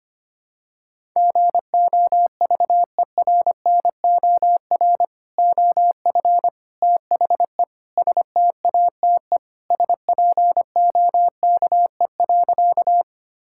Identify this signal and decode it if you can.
Morse code, heard as a keyed tone, about 25 wpm, 700 Hz. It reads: GOVERNOR OF THE STATE SPOKE.
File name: practice / 25wpm / Sentences with 3-Word Phrases - 25wpm